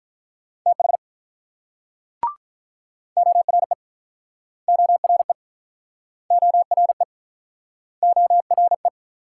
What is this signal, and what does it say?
Morse code, heard as a keyed tone, about 46 wpm, 700 Hz. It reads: TH ORE ORE ORE ORE